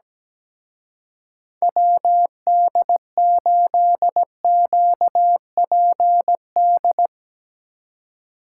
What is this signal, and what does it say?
Morse code, heard as a keyed tone, about 18 wpm, 700 Hz. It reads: WD8QPD
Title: Speed Racing - USA 2x3 Call Signs 17wpm